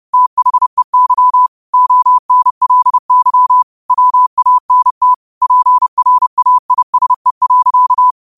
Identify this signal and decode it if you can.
Morse code, heard as a keyed tone, about 30 wpm, 1000 Hz. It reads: THEY ONLY WANT PRAISE.